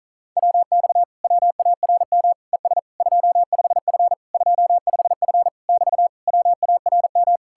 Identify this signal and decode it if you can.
Morse code, heard as a keyed tone, about 48 wpm, 700 Hz. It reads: WX WARM ES 25F 25F <BT> WARM